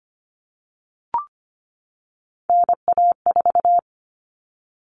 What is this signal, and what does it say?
Morse code, heard as a keyed tone, about 25 wpm, 700 Hz. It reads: NA4
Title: All Cut Numbers - 3-Digit 25wpm